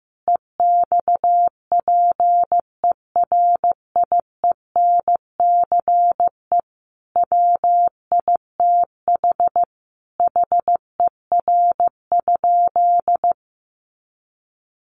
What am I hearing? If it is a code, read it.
Morse code, heard as a keyed tone, about 15 wpm, 700 Hz. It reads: EXPERIENCE WITH HER?
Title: Sentences from Top 600 Words 15wpm